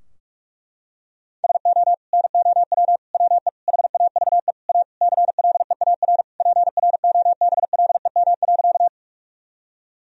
Morse code, heard as a keyed tone, about 47 wpm, 700 Hz. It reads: SO NOW WE HAVE A CLEAR PROBLEM.
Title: Sentences from Top 1000 Words - Review Part 2 - 45wpm